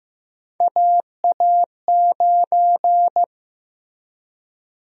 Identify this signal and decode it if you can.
Morse code, heard as a keyed tone, about 15 wpm, 700 Hz. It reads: AA9
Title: USA Call Sign Prefixes - 15wpm